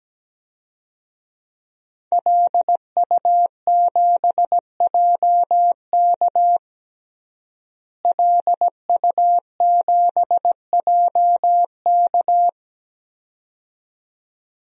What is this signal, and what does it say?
Morse code, heard as a keyed tone, about 17 wpm, 700 Hz. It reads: LU7JK LU7JK
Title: DXCC-100 Call Signs - 5 Characters - Repeated 2x - 17wpm